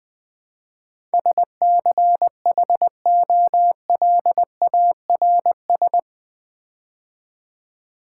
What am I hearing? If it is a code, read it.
Morse code, heard as a keyed tone, about 20 wpm, 700 Hz. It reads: SCHOLARS
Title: Build A Word - 4 Letters At A Time - 11-Letter Words - 20wpm